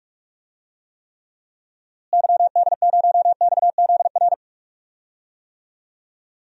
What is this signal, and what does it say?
Morse code, heard as a keyed tone, about 43 wpm, 700 Hz. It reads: YD0XZR
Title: DXCC-100 Call Signs - 6 Characters - Repeated 2x - 45wpm